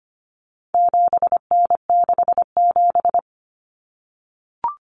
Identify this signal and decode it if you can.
Morse code, heard as a keyed tone, about 25 wpm, 700 Hz. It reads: 7N67
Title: Common Cut Numbers - 4-Digit 25wpm